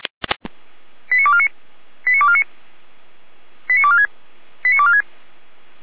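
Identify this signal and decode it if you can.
Five-tone selective call (ZVEI), heard as quick runs of five tones, tones of about 70 ms. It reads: 89249 89249 89246 89246
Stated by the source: ZVEI five-tone call 89229 received on SDR.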